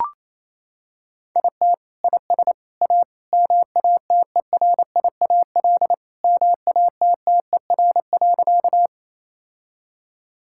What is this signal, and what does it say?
Morse code, heard as a keyed tone, about 28 wpm, 700 Hz. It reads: IT IS A MATERIAL MATTER.
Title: Sentences from Top 1000 Words - Review Part 5 - 28wpm